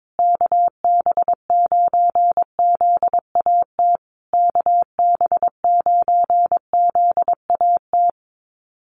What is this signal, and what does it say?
Morse code, heard as a keyed tone, about 22 wpm, 700 Hz. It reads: KB9ZAT KB9ZAT
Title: USA Call Signs Repeated 3x 22wpm